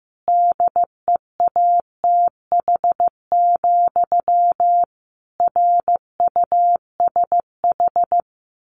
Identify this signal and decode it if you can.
Morse code, heard as a keyed tone, about 15 wpm, 700 Hz. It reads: DEATH, RUSH
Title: A Sound Of Thunder 15wpm